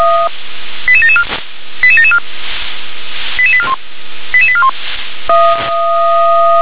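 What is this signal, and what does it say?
Five-tone selective call (ZVEI), heard as quick runs of five tones, tones of about 70 ms. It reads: 88774 88774 88841 88841
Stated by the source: Five-tone ZVEI selective call received with an SDR.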